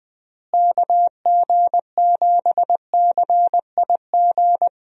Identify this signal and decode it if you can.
Morse code, heard as a keyed tone, about 20 wpm, 700 Hz. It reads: KG7CIG
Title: USA Call Signs - 6 Characters - Repeated 2x - 20wpm